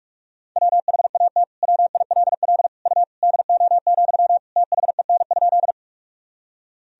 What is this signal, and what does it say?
Morse code, heard as a keyed tone, about 46 wpm, 700 Hz. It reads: WHAT WILL U DO, THEN?
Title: Sentences from Top 100 Words and Abbreviations 45wpm